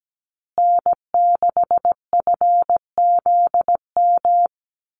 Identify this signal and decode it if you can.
Morse code, heard as a keyed tone, about 17 wpm, 700 Hz. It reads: N6FZM
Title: USA Call Signs Repeated 2x 17wpm